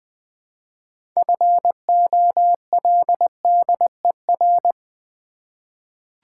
Morse code, heard as a keyed tone, about 20 wpm, 690 Hz. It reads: FOLDER
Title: Difficult Words with Hints - 20wpm